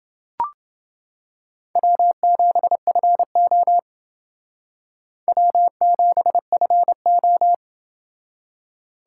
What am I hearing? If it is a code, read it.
Morse code, heard as a keyed tone, about 28 wpm, 700 Hz. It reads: W7FO W7FO